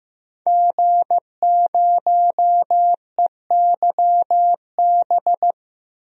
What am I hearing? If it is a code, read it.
Morse code, heard as a keyed tone, about 15 wpm, 700 Hz. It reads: G0EYB
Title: DXCC-100 Call Signs - 5 Characters - 15wpm